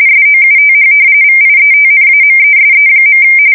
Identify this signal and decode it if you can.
RTTY, about 45.45 baud. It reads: SOUTH VIETNAMESE.
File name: fsk170 / battleshipnj